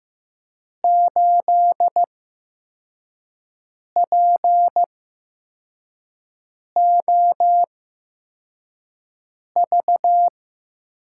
Morse code, heard as a keyed tone, about 15 wpm, spaced as for 4 wpm, 700 Hz. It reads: 8POV
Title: USA Call Signs - 6 Characters - 8x Character Spacing - 15wpm